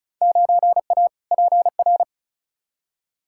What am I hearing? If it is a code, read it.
Morse code, heard as a keyed tone, about 36 wpm, 700 Hz. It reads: 9A PR
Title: Field Day Exchanges Encore - 35wpm